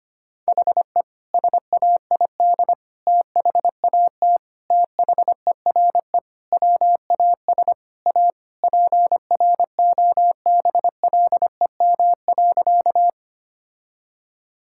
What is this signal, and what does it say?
Morse code, heard as a keyed tone, about 25 wpm, 700 Hz. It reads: HE SAID THAT THERE WAS A PROBLEM.
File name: practice / 25wpm / Sentences with 3-Word Phrases - 25wpm